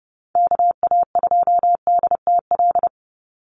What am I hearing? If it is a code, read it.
Morse code, heard as a keyed tone, about 30 wpm, 700 Hz. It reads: KA2DTL
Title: USA Call Signs 30wpm